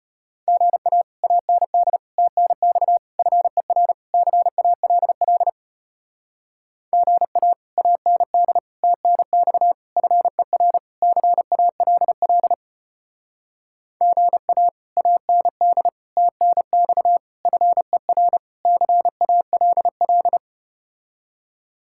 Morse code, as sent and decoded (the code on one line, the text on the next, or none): --. .- / .- -. -.. / - -. -..- / ..-. .-. / -.-. .- .-.. .-.. / --. .- / .- -. -.. / - -. -..- / ..-. .-. / -.-. .- .-.. .-.. / --. .- / .- -. -.. / - -. -..- / ..-. .-. / -.-. .- .-.. .-..
GA AND TNX FER CALL GA AND TNX FER CALL GA AND TNX FER CALL